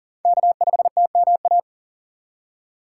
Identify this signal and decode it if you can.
Morse code, heard as a keyed tone, about 39 wpm, 700 Hz. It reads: K5TMA
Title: USA Call Signs - 5 Characters - 3x Character Spacing - 40wpm